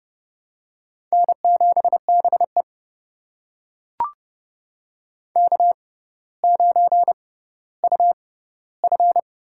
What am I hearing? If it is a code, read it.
Morse code, heard as a keyed tone, about 30 wpm, spaced as for 15 wpm, 700 Hz. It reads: N7BE K9UF
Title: USA Call Signs - 4 Characters - 6x Character Spacing - 30wpm